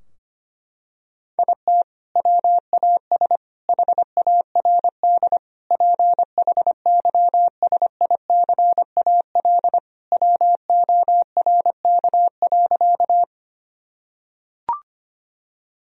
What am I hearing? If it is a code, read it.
Morse code, heard as a keyed tone, about 25 wpm, 700 Hz. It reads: IT WAS HARD PHYSICAL WORK.
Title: Sentences from Top 1000 Words - Review Part 4 - 25wpm